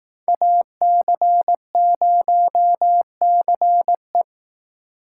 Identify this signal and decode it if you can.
Morse code, heard as a keyed tone, about 18 wpm, 700 Hz. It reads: AC0CE
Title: Speed Racing - USA Call Signs 15wpm